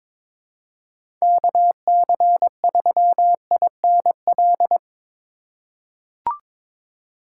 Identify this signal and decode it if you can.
Morse code, heard as a keyed tone, about 22 wpm, 700 Hz. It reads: KC3INL
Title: USA Call Signs - 6 Characters - 3x Character Spacing - 22wpm